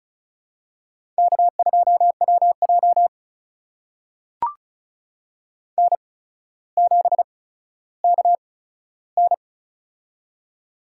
Morse code, heard as a keyed tone, about 34 wpm, spaced as for 15 wpm, 700 Hz. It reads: K2WJ N7KN
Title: USA Call Signs - 4 Characters - 8x Character Spacing - 35wpm